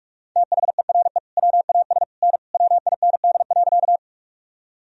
Morse code, heard as a keyed tone, about 46 wpm, 700 Hz. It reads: THERE WAS N WIND.
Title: Sentences from Top 400 Words and Abbreviations 45wpm